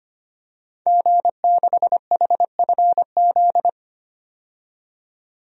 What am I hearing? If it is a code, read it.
Morse code, heard as a keyed tone, about 25 wpm, 700 Hz. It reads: G6HFZ
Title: DXCC-100 Call Signs - 5 Characters - 25wpm